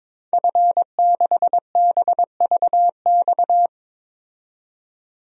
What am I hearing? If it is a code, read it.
Morse code, heard as a keyed tone, about 22 wpm, 700 Hz. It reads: F6BVX
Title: DXCC-100 Call Signs - 5 Characters - 22wpm